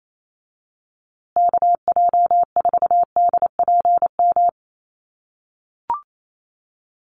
Morse code, heard as a keyed tone, about 28 wpm, 700 Hz. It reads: KJ4DPM
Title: USA Call Signs Repeated 2x 28wpm